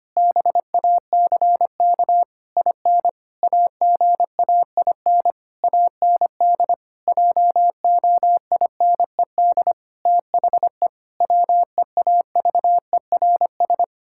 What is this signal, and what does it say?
Morse code, heard as a keyed tone, about 25 wpm, 700 Hz. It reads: BACK IN AGAIN AND JOINED THE WEAVERS
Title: Pangram Challenge v2 - 25wpm